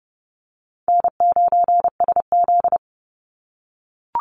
Morse code, heard as a keyed tone, about 30 wpm, 700 Hz. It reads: N9SZ
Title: Easy Call Signs - 30wpm